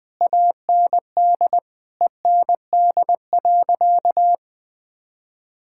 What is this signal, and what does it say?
Morse code, heard as a keyed tone, about 21 wpm, 700 Hz. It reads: AND END.
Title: Sentences from Top 1000 Words - Review Part 6 - 20wpm